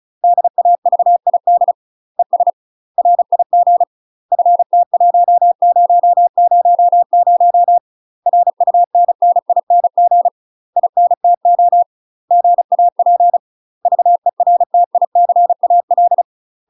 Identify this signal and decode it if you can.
Morse code, heard as a keyed tone, about 34 wpm, 700 Hz. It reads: DAVID ES RIG FT1000 RUNNING INTO GAP VERTICAL
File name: UK MASTER QSOs TRAINING / 35WPM / UK MASTER QSOs 0021